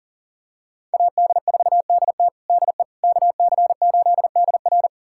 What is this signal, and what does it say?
Morse code, heard as a keyed tone, about 40 wpm, 700 Hz. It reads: AD4DT DE KC8DR
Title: USA Callsign DE Callsign - 5 Characters - 40wpm